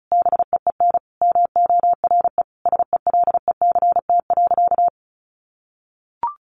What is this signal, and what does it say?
Morse code, heard as a keyed tone, about 34 wpm, 700 Hz. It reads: BEEN MORE SELECT.
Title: Sentences from Top 1300 Words 35wpm